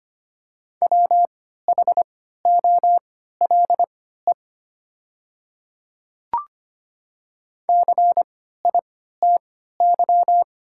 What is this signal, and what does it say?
Morse code, heard as a keyed tone, about 25 wpm, spaced as for 14 wpm, 700 Hz. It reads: WHOLE CITY
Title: 3x Character Spacing - Top 500 Words - 25wpm